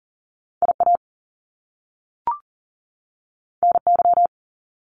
Morse code, heard as a keyed tone, about 41 wpm, 700 Hz. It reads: IA NY
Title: US States 40wpm